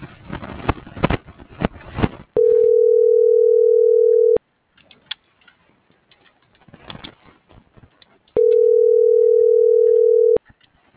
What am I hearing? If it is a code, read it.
Ringback tone (North America), 440 + 480 Hz.